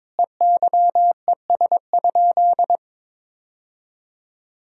Morse code, heard as a keyed tone, about 22 wpm, 700 Hz. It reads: EYES?
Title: Sentences from Top 1000 Words - Review Part 4 - 22wpm